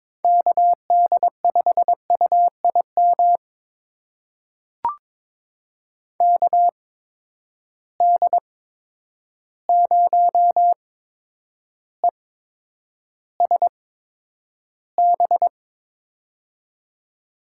Morse code, heard as a keyed tone, about 22 wpm, spaced as for 9 wpm, 700 Hz. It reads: KD5UIM KD0ESB
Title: USA Call Signs - 6 Characters - 8x Character Spacing - 22wpm